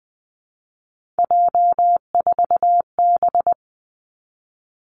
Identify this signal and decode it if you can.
Morse code, heard as a keyed tone, about 19 wpm, 700 Hz. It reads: J4B
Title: Three Challenging Letter-Number Combinations 20wpm